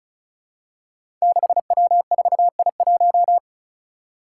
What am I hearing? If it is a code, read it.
Morse code, heard as a keyed tone, about 36 wpm, 700 Hz. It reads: 6W4I1
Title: Five Challenging Letter-Number Combinations 35wpm